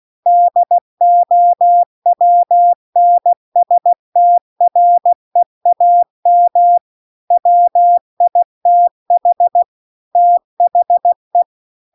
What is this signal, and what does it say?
Morse code, heard as a keyed tone, about 16 wpm, 700 Hz. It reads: DOWNSTREAM WITH THE
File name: CW-The-Big-Twoheaded-River-2-16WPM